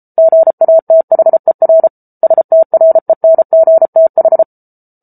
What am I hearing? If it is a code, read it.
Morse code, heard as a keyed tone, about 34 wpm, 650 Hz. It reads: GATHER STRENGTH